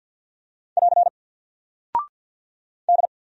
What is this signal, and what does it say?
Morse code, heard as a keyed tone, about 48 wpm, 700 Hz. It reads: <AR> D
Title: Single Character (Letter:Number:Punctuation:Prosign) 50wpm